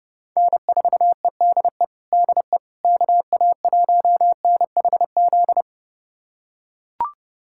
Morse code, heard as a keyed tone, about 30 wpm, 700 Hz. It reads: N4EDE DE KA1NHZ
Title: USA Callsign DE Callsign 30wpm